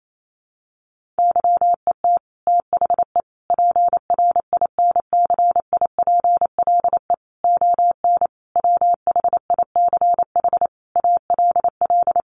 Morse code, heard as a keyed tone, about 28 wpm, 700 Hz. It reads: YET THE PRINCIPLE ON WHICH ALL
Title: Sentences from Top 1400 Words 28wpm